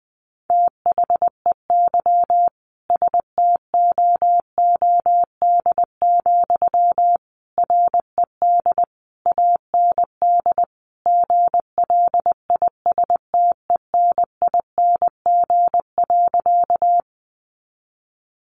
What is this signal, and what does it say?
Morse code, heard as a keyed tone, about 20 wpm, 700 Hz. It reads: THEY STOOD, RED AND GLISTENING.
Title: A Sound Of Thunder 20wpm